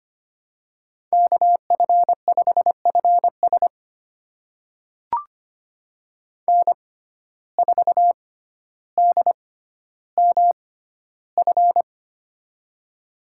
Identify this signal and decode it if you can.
Morse code, heard as a keyed tone, about 25 wpm, spaced as for 13 wpm, 690 Hz. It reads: KF5FS N4DMF
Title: USA Call Signs - 5 Characters - 6x Character Spacing - 25wpm